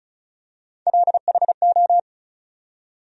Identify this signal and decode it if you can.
Morse code, heard as a keyed tone, about 36 wpm, 700 Hz. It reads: LHO
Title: USA Call Sign Postfixes - 3 Letters 35wpm